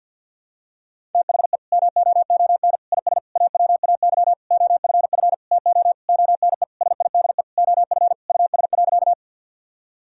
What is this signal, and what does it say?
Morse code, heard as a keyed tone, about 46 wpm, 700 Hz. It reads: THE MOON IS AWAY OFF TO ONE SIDE OF US.
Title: Sentences from Top 1000 Words - Review Part 3 - 50wpm